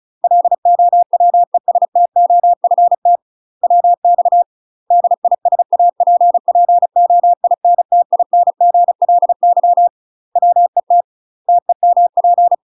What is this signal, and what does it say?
Morse code, heard as a keyed tone, about 36 wpm, 700 Hz. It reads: LOWESTOFT WX DISAPPOINTINGLY WET TEMP
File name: UK MASTER QSOs TRAINING / 35WPM / UK MASTER QSOs 0029